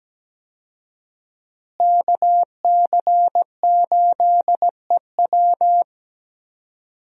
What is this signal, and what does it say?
Morse code, heard as a keyed tone, about 17 wpm, 700 Hz. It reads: KC8EW
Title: USA Call Signs - 5 Characters - Repeated 3x - 17wpm